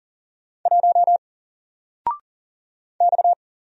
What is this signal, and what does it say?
Morse code, heard as a keyed tone, about 40 wpm, 700 Hz. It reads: 1 X
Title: Single Letter-Number 40wpm